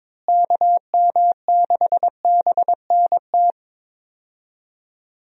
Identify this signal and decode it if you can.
Morse code, heard as a keyed tone, about 23 wpm, 700 Hz. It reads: KM6BNT
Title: USA Call Signs - 6 Characters - Repeated 2x - 22wpm